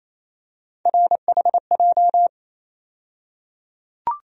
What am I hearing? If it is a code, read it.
Morse code, heard as a keyed tone, about 28 wpm, 700 Hz. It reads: RHJ
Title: USA Call Sign Postfixes - 3 Letters 28wpm